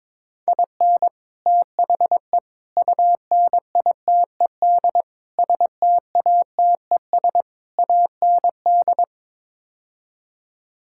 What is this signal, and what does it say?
Morse code, heard as a keyed tone, about 22 wpm, 700 Hz. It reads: IN THE UNITED STATES AND
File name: practice / 22wpm / Build A Phrase - 5-Word Phrases - 22wpm